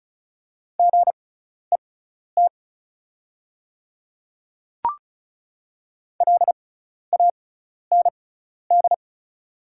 Morse code, heard as a keyed tone, about 36 wpm, spaced as for 12 wpm, 700 Hz. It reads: GET LAND
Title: 6x Character Spacing - Top 200 Words - 35wpm